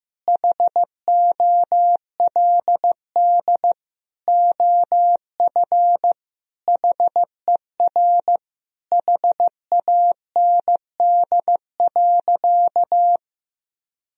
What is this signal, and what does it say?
Morse code, heard as a keyed tone, about 15 wpm, 700 Hz. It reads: HOLD OF HER HAND.
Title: Sentences from Top 1000 Words - Review Part 2 - 15wpm